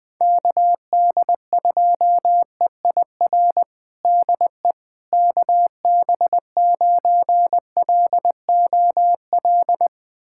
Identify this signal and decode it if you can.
Morse code, heard as a keyed tone, about 20 wpm, 700 Hz. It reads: KD2EIR DE KB9LOL
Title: USA Callsign DE Callsign - 6 Characters - 20wpm